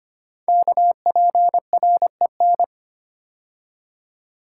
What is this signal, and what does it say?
Morse code, heard as a keyed tone, about 25 wpm, 700 Hz. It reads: KPREN
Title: Code Groups - 5 Letters - 25wpm